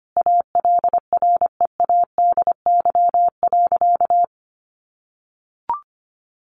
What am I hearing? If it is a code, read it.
Morse code, heard as a keyed tone, about 25 wpm, 700 Hz. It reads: ALREADY.
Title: Sentences from Top 900 Words 25wpm